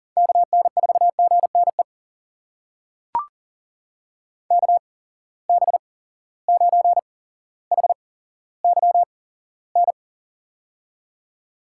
Morse code, heard as a keyed tone, about 38 wpm, spaced as for 17 wpm, 700 Hz. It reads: KN4GNE KB9HYN
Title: USA Call Signs - 6 Characters - 8x Character Spacing - 40wpm